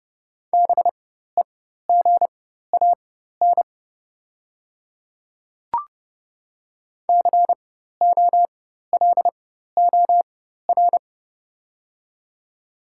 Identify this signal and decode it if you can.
Morse code, heard as a keyed tone, about 30 wpm, spaced as for 14 wpm, 700 Hz. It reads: BEGAN COLOR
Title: 4x Character Spacing - Top 300 Words - 30wpm